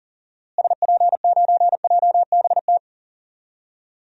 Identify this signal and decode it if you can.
Morse code, heard as a keyed tone, about 40 wpm, 700 Hz. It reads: SP9JBT
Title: DXCC-100 Call Signs - 6 Characters - Repeated 3x - 40wpm